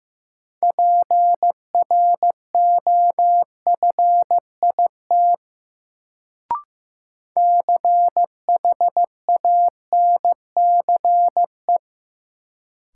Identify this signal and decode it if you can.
Morse code, heard as a keyed tone, about 15 wpm, 700 Hz. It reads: PROFIT CHANCE
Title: Sentences from Top 800 Words - New Word Drill - 15wpm